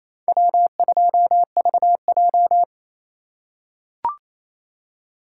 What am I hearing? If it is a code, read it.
Morse code, heard as a keyed tone, about 28 wpm, 700 Hz. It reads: W2VJ
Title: USA Call Signs - 4 Characters - 8x Character Spacing - 28wpm